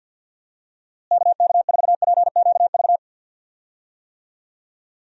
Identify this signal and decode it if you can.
Morse code, heard as a keyed tone, about 52 wpm, 700 Hz. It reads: KK4PQV
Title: USA Call Signs - 6 Characters - Repeated 2x - 50wpm